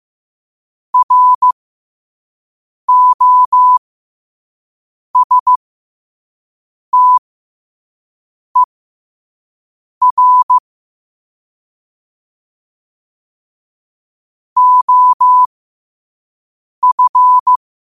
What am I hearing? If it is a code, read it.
Morse code, heard as a keyed tone, about 15 wpm, spaced as for 5 wpm, 1000 Hz. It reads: ROSTER OF
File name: qotdfiles / 5wpm